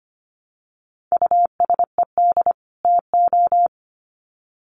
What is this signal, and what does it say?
Morse code, heard as a keyed tone, about 25 wpm, 700 Hz. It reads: USED TO
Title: Sets of 2 Words 25wpm